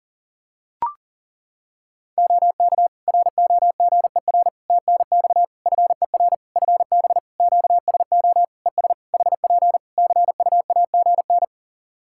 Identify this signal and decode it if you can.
Morse code, heard as a keyed tone, about 41 wpm, 700 Hz. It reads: OK ROGER TNX FER FB QSO ES HP CUAGN
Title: Practice QSO - Exchange 4 - Ending - 40wpm